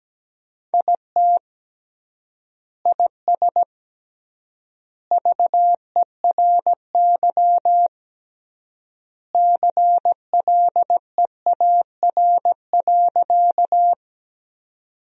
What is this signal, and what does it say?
Morse code, heard as a keyed tone, about 17 wpm, 700 Hz. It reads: IT IS VERY CLEAR.
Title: 3x Word-Spacing - Sentences from Top 500 Words - 17wpm